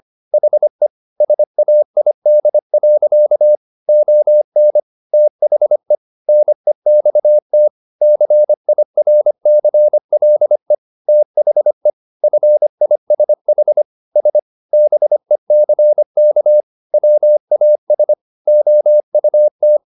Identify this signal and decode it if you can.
Morse code, heard as a keyed tone, about 25 wpm, 600 Hz. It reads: HE SAID. ON THE NEXT CIRCLE THE FISH S BECK WAS OUT